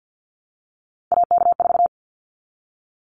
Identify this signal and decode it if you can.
Morse code, heard as a keyed tone, about 49 wpm, 700 Hz. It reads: AK4
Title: Callsign Prefixes for Top 25 DXCC - Repeated 3x - 50wpm